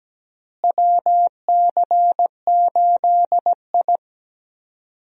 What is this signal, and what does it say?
Morse code, heard as a keyed tone, about 17 wpm, 700 Hz. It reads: WC8I
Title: USA Call Signs - 4 Characters - Repeated 2x - 17wpm